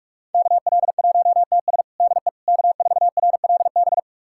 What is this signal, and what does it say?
Morse code, heard as a keyed tone, about 46 wpm, 700 Hz. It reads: KL1TS DE K4RLB
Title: USA Callsign DE Callsign 45wpm